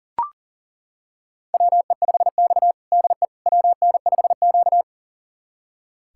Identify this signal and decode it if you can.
Morse code, heard as a keyed tone, about 40 wpm, 700 Hz. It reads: WE5X DE WN5Q
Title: USA Callsign DE Callsign - 4 Characters - 40wpm